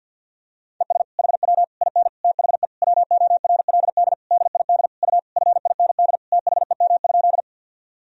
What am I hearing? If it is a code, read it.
Morse code, heard as a keyed tone, about 52 wpm, 700 Hz. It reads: ES HW IN THE WORLD DID U FIND THEM?
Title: Sentences from Top 200 Words and Abbreviations 50wpm